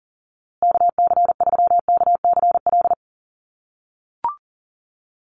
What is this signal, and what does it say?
Morse code, heard as a keyed tone, about 40 wpm, 700 Hz. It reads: KC3KCL
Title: USA Call Signs Repeated 3x 40wpm